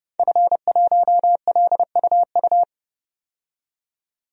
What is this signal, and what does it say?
Morse code, heard as a keyed tone, about 31 wpm, 700 Hz. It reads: F1LUU
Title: DXCC-100 Call Signs - 5 Characters - 30wpm